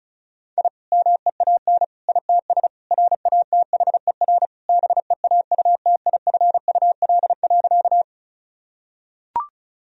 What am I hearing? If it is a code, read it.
Morse code, heard as a keyed tone, about 36 wpm, 700 Hz. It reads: I MEAN ITS RATHER BEAUTIFUL.
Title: Sentences from Top 1000 Words - Review Part 4 - 35wpm